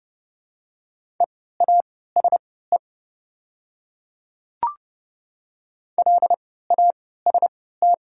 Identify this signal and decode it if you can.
Morse code, heard as a keyed tone, about 30 wpm, spaced as for 17 wpm, 700 Hz. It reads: EASE LAST
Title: 3x Character Spacing - Top 300 Words - 30wpm